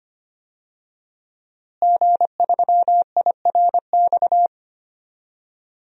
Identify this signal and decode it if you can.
Morse code, heard as a keyed tone, about 25 wpm, 700 Hz. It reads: G3IRX
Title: DXCC-100 Call Signs - 5 Characters - Repeated 3x - 25wpm